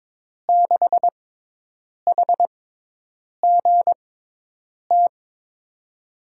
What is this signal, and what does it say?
Morse code, heard as a keyed tone, about 22 wpm, spaced as for 8 wpm, 700 Hz. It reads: 6HGT
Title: USA Call Signs - 5 Characters - 6x Character Spacing - 22wpm